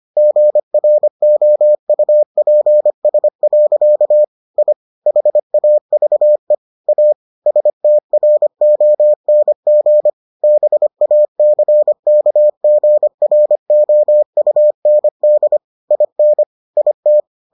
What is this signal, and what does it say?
Morse code, heard as a keyed tone, about 25 wpm, 600 Hz. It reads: GROUPS. I HAVE A STRONG BACKGROUND IN IT